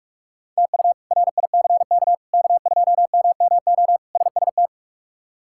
Morse code, heard as a keyed tone, about 42 wpm, 700 Hz. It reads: TU RICK K2MMO SST
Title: K1USN SST QSO Practice - 45wpm